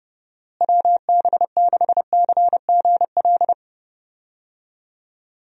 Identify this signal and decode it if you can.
Morse code, heard as a keyed tone, about 30 wpm, 700 Hz. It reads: WB6CGL
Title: USA Call Signs - 6 Characters - Repeated 2x - 30wpm